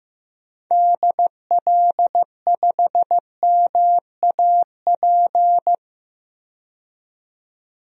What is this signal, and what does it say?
Morse code, heard as a keyed tone, about 15 wpm, 700 Hz. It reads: DL5MAP
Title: DXCC-100 Call Signs - 6 Characters - 15wpm